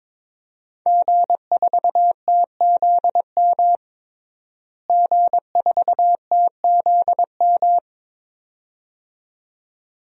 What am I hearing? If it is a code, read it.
Morse code, heard as a keyed tone, about 22 wpm, 700 Hz. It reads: G4TZM G4TZM
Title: DXCC-100 Call Signs - 5 Characters - Repeated 2x - 22wpm